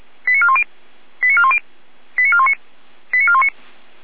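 Five-tone selective call (ZVEI), heard as quick runs of five tones, tones of about 70 ms. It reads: 89410 89410 89419 89419
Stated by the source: ZVEI five-tone selective call received on SDR.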